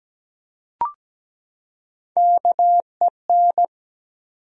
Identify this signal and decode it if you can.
Morse code, heard as a keyed tone, about 17 wpm, 700 Hz. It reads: KEN
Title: USA Common Names - Max 3 Letters - 17wpm